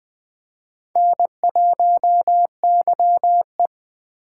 Morse code, heard as a keyed tone, about 20 wpm, 700 Hz. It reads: N1YE
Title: Speed Racing - USA 1x2 Call Signs 17wpm